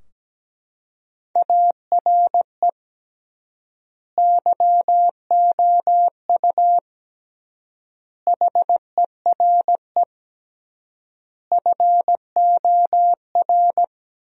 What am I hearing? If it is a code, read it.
Morse code, heard as a keyed tone, about 17 wpm, 700 Hz. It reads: ARE YOU HERE FOR
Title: 3x Word-Spacing - Sentences from Top 200 Words - 17wpm